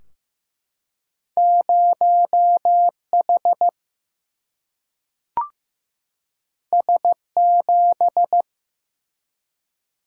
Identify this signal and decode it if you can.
Morse code, heard as a keyed tone, about 15 wpm, 700 Hz. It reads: 0H S7
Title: Single-Letter Single-Number 15wpm